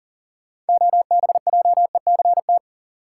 Morse code, heard as a keyed tone, about 38 wpm, 700 Hz. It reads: OBJECT
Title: Sentences from Top 500 Words - New Word Drill - 40wpm